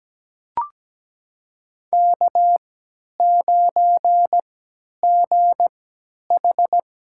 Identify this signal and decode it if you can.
Morse code, heard as a keyed tone, about 17 wpm, spaced as for 10 wpm, 700 Hz. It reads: K9GH
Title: USA Call Signs - 3x Character Spacing - 17wpm